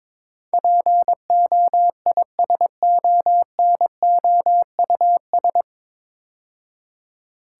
Morse code, heard as a keyed tone, about 22 wpm, 700 Hz. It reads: POISONOUS
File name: 9 Letter Words - First 3 Letters Given - 22wpm